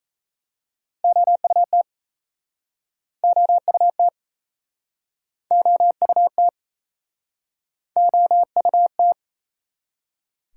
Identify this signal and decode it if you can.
Morse code, heard as a keyed tone, about 38 wpm, 700 Hz. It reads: OUT OUT OUT OUT